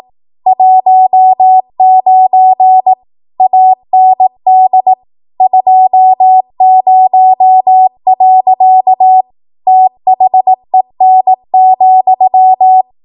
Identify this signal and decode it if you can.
Morse code, heard as a keyed tone, about 18 wpm, 750 Hz. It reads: 19 AND 20. THEN,
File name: Archive / 18 WPM / 18WPM